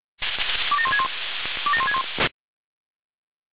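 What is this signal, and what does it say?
Five-tone selective call (ZVEI), heard as quick runs of five tones, tones of about 70 ms. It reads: 27161 27161
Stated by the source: ZVEI five-tone call 27001 received by SDR.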